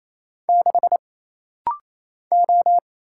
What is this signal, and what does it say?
Morse code, heard as a keyed tone, about 27 wpm, 700 Hz. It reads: 6 O
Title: Single Letter-Number Alt - Rapid-Fire 28wpm